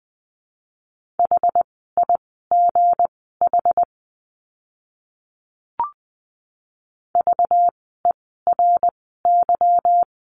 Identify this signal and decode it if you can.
Morse code, heard as a keyed tone, about 20 wpm, spaced as for 14 wpm, 700 Hz. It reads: HIGH VERY
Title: 237 Lesson 38f - Letter q - Words 20wpm